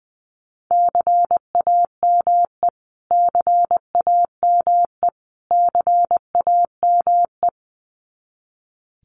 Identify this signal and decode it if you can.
Morse code, heard as a keyed tone, about 20 wpm, 700 Hz. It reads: CAME CAME CAME
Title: Top 500 Words Repeated 20wpm